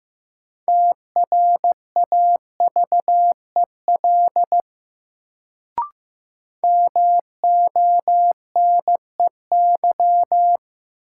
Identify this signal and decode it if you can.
Morse code, heard as a keyed tone, about 15 wpm, 700 Hz. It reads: TRAVEL MONEY
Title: Sentences from Top 400 Words - New Word Drill - 15wpm